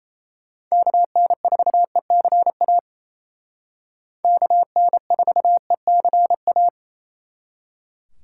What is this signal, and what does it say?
Morse code, heard as a keyed tone, about 36 wpm, 700 Hz. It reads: KN4ECA KN4ECA